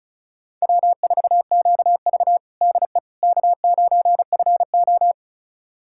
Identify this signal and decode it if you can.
Morse code, heard as a keyed tone, about 36 wpm, 700 Hz. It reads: W4QV DE K9FO
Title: USA Callsign DE Callsign - 4 Characters - 35wpm